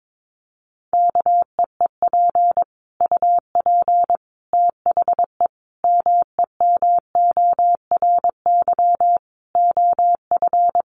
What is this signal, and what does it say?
Morse code, heard as a keyed tone, about 22 wpm, 700 Hz. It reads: KEEP UP THE MEMORY OF
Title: Sentences from Top 1300 Words 22wpm